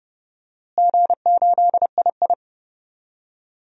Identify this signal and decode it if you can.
Morse code, heard as a keyed tone, about 30 wpm, 700 Hz. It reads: G8II
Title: DXCC-100 Call Signs - 4 Characters - Repeated 3x - 30wpm